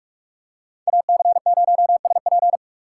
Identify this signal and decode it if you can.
Morse code, heard as a keyed tone, about 45 wpm, 700 Hz. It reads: AC0SP
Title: USA Call Signs - 5 Characters - 3x Character Spacing - 45wpm